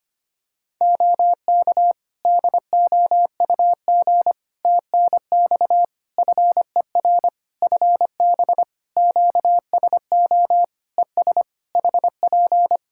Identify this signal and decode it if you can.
Morse code, heard as a keyed tone, about 25 wpm, 700 Hz. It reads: OK DOUG TNX FER FB QSO ES HP
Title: Practice QSO - Exchange 4 - Ending - 25wpm